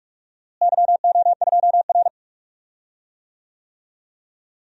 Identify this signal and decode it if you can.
Morse code, heard as a keyed tone, about 46 wpm, 700 Hz. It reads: YO2R
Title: DXCC-100 Call Signs - 4 Characters - Repeated 3x - 45wpm